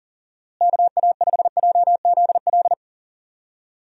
Morse code, heard as a keyed tone, about 40 wpm, 700 Hz. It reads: KA5JZL
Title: USA Call Signs - 6 Characters - Repeated 2x - 40wpm